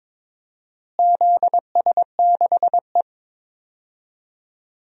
Morse code, heard as a keyed tone, about 22 wpm, 700 Hz. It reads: ZS6E
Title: DXCC-100 Call Signs - 4 Characters - Repeated 2x - 22wpm